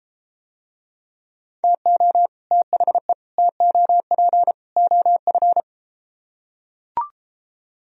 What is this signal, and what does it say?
Morse code, heard as a keyed tone, about 33 wpm, 700 Hz. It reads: TO THE TOP OF